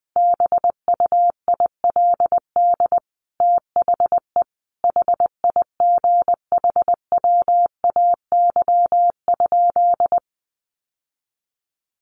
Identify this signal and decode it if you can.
Morse code, heard as a keyed tone, about 20 wpm, 700 Hz. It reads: BUILD THE HIGHWAY?
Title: Sentences from Top 1400 Words 20wpm